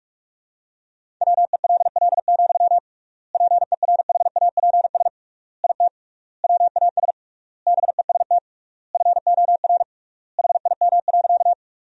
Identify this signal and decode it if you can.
Morse code, heard as a keyed tone, about 46 wpm, 700 Hz. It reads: WELL, PERHAPS IT WAS BEST FOR HIM.
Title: 3x Word-Spacing - Sentences from Top 500 Words - 45wpm